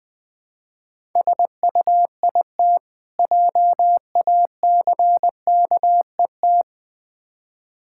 Morse code, heard as a keyed tone, about 20 wpm, 700 Hz. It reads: SUIT JACKET
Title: Difficult Words - Speed-Racing with Hints - 20wpm